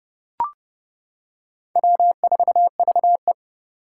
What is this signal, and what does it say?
Morse code, heard as a keyed tone, about 30 wpm, 700 Hz. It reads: W4VE
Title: Four Challenging Letter-Number Combinations 30wpm